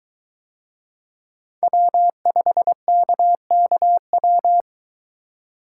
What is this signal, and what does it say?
Morse code, heard as a keyed tone, about 22 wpm, 700 Hz. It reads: W5KKW